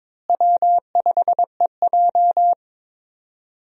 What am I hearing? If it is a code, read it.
Morse code, heard as a keyed tone, about 22 wpm, 700 Hz. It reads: W5EJ
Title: USA Call Signs - 4 Characters - Repeated 3x - 22wpm